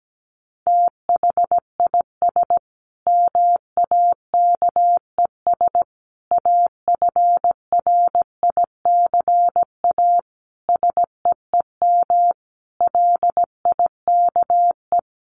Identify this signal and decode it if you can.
Morse code, heard as a keyed tone, about 17 wpm, 700 Hz. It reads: THIS MAKES AFRICA SEEM LIKE
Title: A Sound Of Thunder 17wpm